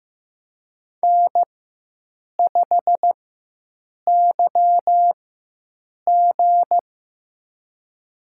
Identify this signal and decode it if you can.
Morse code, heard as a keyed tone, about 15 wpm, spaced as for 7 wpm, 700 Hz. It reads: N5YG
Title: USA Call Signs - 4x Character Spacing - 15wpm